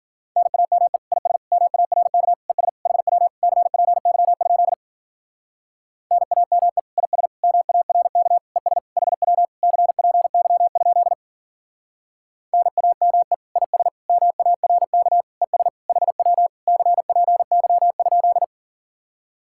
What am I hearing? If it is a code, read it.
Morse code, heard as a keyed tone, about 50 wpm, 700 Hz. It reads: NAME IS MARK ES HW CPY? NAME IS MARK ES HW CPY? NAME IS MARK ES HW CPY?
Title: Speed-Racing Common QSO Phrases - 40wpm